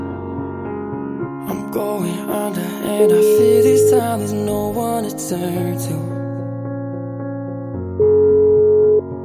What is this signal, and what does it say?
Ringback tone (Europe, 425 Hz).